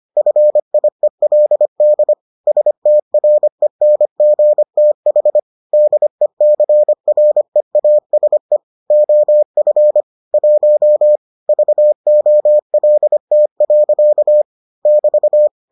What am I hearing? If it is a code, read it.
Morse code, heard as a keyed tone, about 25 wpm, 600 Hz. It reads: FIELD STRENGTH DECREASE OF 1 VOLT. <BT>